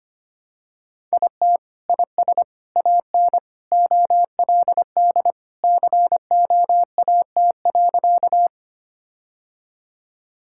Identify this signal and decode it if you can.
Morse code, heard as a keyed tone, about 25 wpm, 700 Hz. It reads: IT IS AN OLD COAT.
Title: Sentences from Top 1600 Words 25wpm